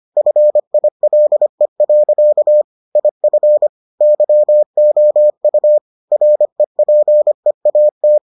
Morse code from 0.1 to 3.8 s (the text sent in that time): FILE. IF